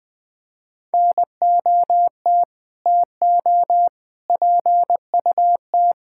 Morse code, heard as a keyed tone, about 20 wpm, 700 Hz. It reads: NOT TO PUT
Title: Sets of 9 Words - 20wpm